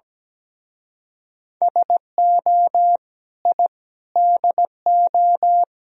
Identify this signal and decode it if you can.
Morse code, heard as a keyed tone, about 17 wpm, 700 Hz. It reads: SO I DO
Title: Speed-Racing - Sets of 5 Words 17wpm